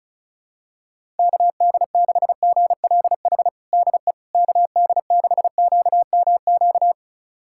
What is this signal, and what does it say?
Morse code, heard as a keyed tone, about 34 wpm, 700 Hz. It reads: KD6GLH DE KD6QMQ
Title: USA Callsign DE Callsign - 6 Characters - 35wpm